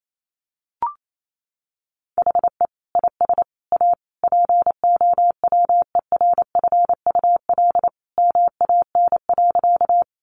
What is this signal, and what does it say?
Morse code, heard as a keyed tone, about 28 wpm, 700 Hz. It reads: HE IS A POWERFUL MAN.
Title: Sentences from Top 1300 Words 28wpm